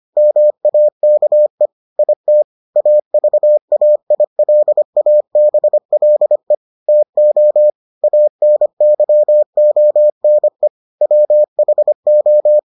Morse code, heard as a keyed tone, about 25 wpm, 600 Hz. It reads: MAKE IT AVAILABLE TO ANYONE WHO